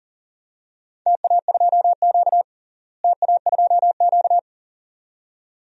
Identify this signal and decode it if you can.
Morse code, heard as a keyed tone, about 40 wpm, 700 Hz. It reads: TA2Q TA2Q
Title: DXCC-100 Call Signs - 4 Characters - Repeated 2x - 40wpm